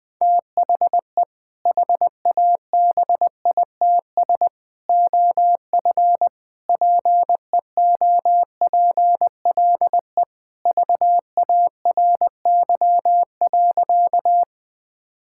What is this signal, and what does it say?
Morse code, heard as a keyed tone, about 20 wpm, 700 Hz. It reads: THE HABITS OF PEOPLE VARY.
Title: Sentences with 2-Word Phrases - Spoken Beginning - 20wpm